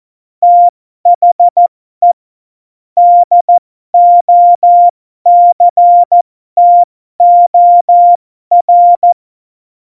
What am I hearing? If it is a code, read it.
Morse code, heard as a keyed tone, about 14 wpm, spaced as for 12 wpm, 700 Hz. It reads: THE DOCTOR
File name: Morsekurs / Witze